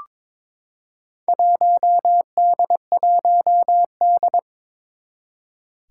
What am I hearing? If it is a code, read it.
Morse code, heard as a keyed tone, about 22 wpm, 700 Hz. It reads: 1D1D
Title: Four Challenging Letter-Number Combinations 22wpm